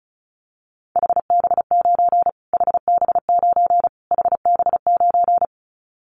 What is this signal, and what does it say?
Morse code, heard as a keyed tone, about 34 wpm, 700 Hz. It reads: HB9 HB9 HB9
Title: Callsign Prefixes for Top 25 DXCC - Repeated 3x - 35wpm